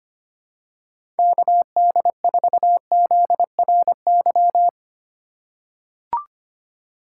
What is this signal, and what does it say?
Morse code, heard as a keyed tone, about 25 wpm, 700 Hz. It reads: KD4ZRY
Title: USA Call Signs - 6 Characters - 8x Character Spacing - 25wpm